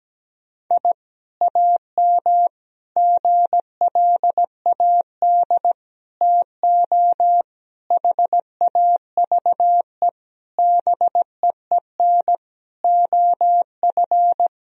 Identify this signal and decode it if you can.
Morse code, heard as a keyed tone, about 17 wpm, 700 Hz. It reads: I AM GLAD TO HAVE BEEN OF
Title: Sentences from Top 1000 Words - Review Part 4 - 17wpm